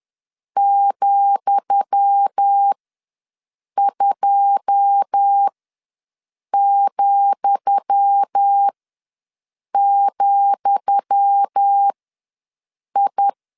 Morse code, heard as a keyed tone, about 10 wpm, spaced as for 6 wpm, 800 Hz. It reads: ,2,,I
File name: LEZIONI / 10 SIGNS / 10D COMA ws 10wpm 4314